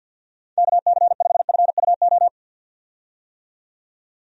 Morse code, heard as a keyed tone, about 52 wpm, 700 Hz. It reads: KC5FUO
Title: USA Call Signs - 6 Characters - 50wpm